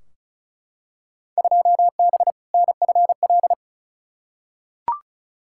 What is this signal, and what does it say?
Morse code, heard as a keyed tone, about 36 wpm, 690 Hz. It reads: 2B NFL
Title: Field Day Exchanges Encore - 35wpm